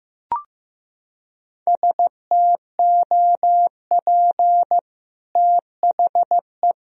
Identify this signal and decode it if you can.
Morse code, heard as a keyed tone, about 15 wpm, 700 Hz. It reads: STOP THE
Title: Sentences from Top 1000 Words - Review Part 1 - 15wpm